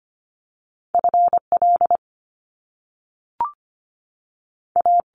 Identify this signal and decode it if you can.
Morse code, heard as a keyed tone, about 25 wpm, spaced as for 3 wpm, 700 Hz. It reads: F LA
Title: Top Two Letter Combinations in Words 25wpm